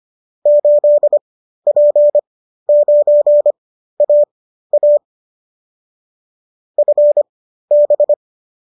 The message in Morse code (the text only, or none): 8P9AA FB